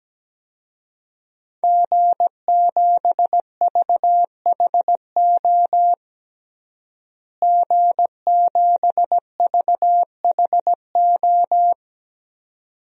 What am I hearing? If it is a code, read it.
Morse code, heard as a keyed tone, about 17 wpm, 700 Hz. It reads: G7VHO G7VHO
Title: DXCC-100 Call Signs - 5 Characters - Repeated 2x - 17wpm